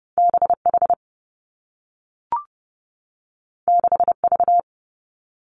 Morse code, heard as a keyed tone, about 30 wpm, 700 Hz. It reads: B H 6 V
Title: Two Challenging Letter-Number Combinations 30wpm